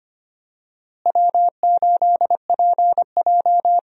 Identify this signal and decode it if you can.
Morse code, heard as a keyed tone, about 25 wpm, 700 Hz. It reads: W8PJ
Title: USA Call Signs - 4 Characters - Repeated 3x - 25wpm